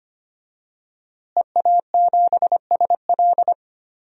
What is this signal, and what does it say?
Morse code, heard as a keyed tone, about 25 wpm, 700 Hz. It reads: EA7SL
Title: DXCC-100 Call Signs - 5 Characters - Repeated 3x - 25wpm